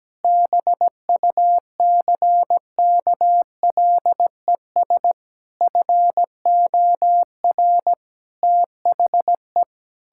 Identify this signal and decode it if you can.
Morse code, heard as a keyed tone, about 17 wpm, 700 Hz. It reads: BUCKLES FOR THE
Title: Pangram Challenge v2 - 17wpm